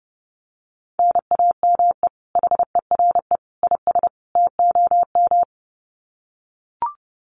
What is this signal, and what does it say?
Morse code, heard as a keyed tone, about 30 wpm, 700 Hz. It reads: NAME HERE IS TOM
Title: Common QSO Phrases 30wpm